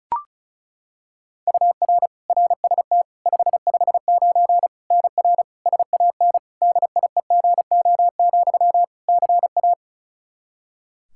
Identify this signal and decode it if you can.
Morse code, heard as a keyed tone, about 36 wpm, 700 Hz. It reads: UR RST 559 NR SAN DIEGO, CA
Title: Practice QSO - Exchange 1 - Beginning - 35wpm